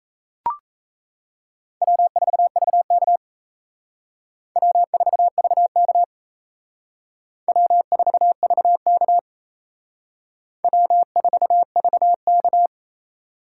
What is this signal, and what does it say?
Morse code, heard as a keyed tone, about 37 wpm, 700 Hz. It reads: W4VK W4VK W4VK W4VK